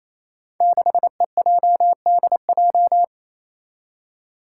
Morse code, heard as a keyed tone, about 28 wpm, 700 Hz. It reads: 6EJDJ
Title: Five Challenging Letter-Number Combinations 28wpm